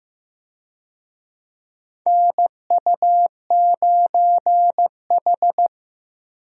Morse code, heard as a keyed tone, about 15 wpm, 700 Hz. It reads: NU9H
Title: USA Call Signs - 4 Characters - Repeated 3x - 15wpm